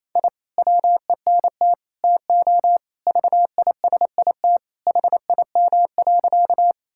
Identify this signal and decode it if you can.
Morse code, heard as a keyed tone, about 28 wpm, 700 Hz. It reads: I WENT TO VISIT HIM.
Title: Sentences from Top 1000 Words - Review Part 6 - 28wpm